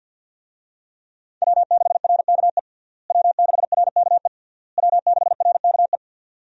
Morse code, heard as a keyed tone, about 51 wpm, 700 Hz. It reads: W6RKE W6RKE W6RKE
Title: USA Call Signs - 5 Characters - Repeated 3x - 50wpm